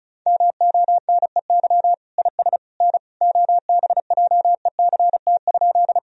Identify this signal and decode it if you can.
Morse code, heard as a keyed tone, about 34 wpm, 700 Hz. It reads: MONEY IS N OBJECT?
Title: Sentences from Top 500 Words and Abbreviations 35wpm